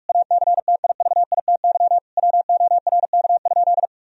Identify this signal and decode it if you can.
Morse code, heard as a keyed tone, about 41 wpm, 700 Hz. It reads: ACTIVITY WORK?
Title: Sentences from Top 1000 Words - Review Part 4 - 45wpm